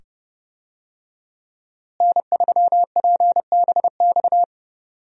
Morse code, heard as a keyed tone, about 30 wpm, 700 Hz. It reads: N3PBX
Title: USA Call Signs - 5 Characters - Repeated 2x - 30wpm